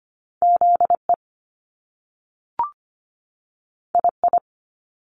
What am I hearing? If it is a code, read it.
Morse code, heard as a keyed tone, about 25 wpm, spaced as for 3 wpm, 700 Hz. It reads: Z EI I
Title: Top Two Letter Combinations in Words 25wpm